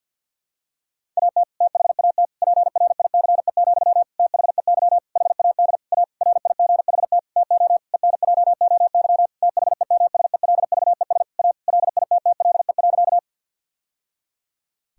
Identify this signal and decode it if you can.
Morse code, heard as a keyed tone, about 52 wpm, 700 Hz. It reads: AT THAT PRICE, THEY HAD A RIGHT TO ENJOY THEMSELVES A LITTLE.
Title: Sentences from Top 1000 Words - Review Part 5 - 50wpm